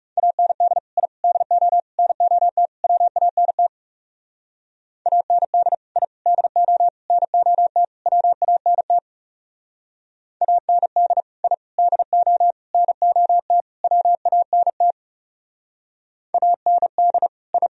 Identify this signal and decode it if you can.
Morse code, heard as a keyed tone, about 36 wpm, 700 Hz. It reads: AND I DO NOT WANT AND I DO NOT WANT AND I DO NOT WANT AND I
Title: Speed-Racing - Sets of 5 Words 30wpm